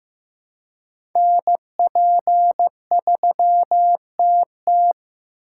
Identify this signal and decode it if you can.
Morse code, heard as a keyed tone, about 15 wpm, 700 Hz. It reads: NP3TT
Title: USA Call Signs - 4x Character Spacing - 15wpm